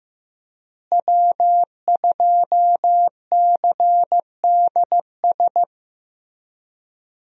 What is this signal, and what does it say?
Morse code, heard as a keyed tone, about 15 wpm, 700 Hz. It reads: W2CDS
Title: USA Call Signs - 5 Characters - Repeated 3x - 15wpm